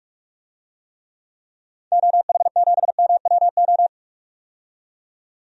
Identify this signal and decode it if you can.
Morse code, heard as a keyed tone, about 46 wpm, 700 Hz. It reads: OH7MWO
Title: DXCC-100 Call Signs - 6 Characters - 45wpm